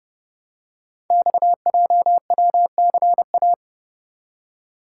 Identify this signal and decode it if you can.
Morse code, heard as a keyed tone, about 30 wpm, 700 Hz. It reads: XJWCA
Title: Code Groups - 5 Letter:Numbers - 30wpm